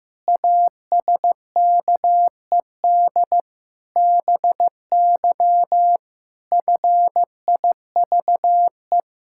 Morse code, heard as a keyed tone, about 15 wpm, 700 Hz. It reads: ASKED BY FIVE
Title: Pangram Challenge v2 - 15wpm